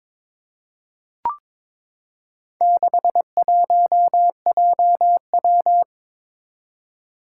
Morse code, heard as a keyed tone, about 22 wpm, 700 Hz. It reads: 61JW
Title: Four Challenging Letter-Number Combinations 22wpm